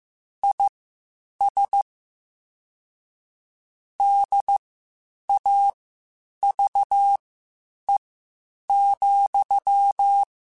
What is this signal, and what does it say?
Morse code, heard as a keyed tone, about 15 wpm, spaced as for 8 wpm, 800 Hz. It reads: IS DAVE,
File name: (spaced) / QSOs Spaced-mp3s / kc7haa @13 sp